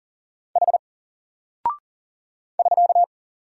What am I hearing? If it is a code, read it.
Morse code, heard as a keyed tone, about 39 wpm, 700 Hz. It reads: H <SK>
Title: Single Character (Letter:Number:Punctuation:Prosign) Alt 40wpm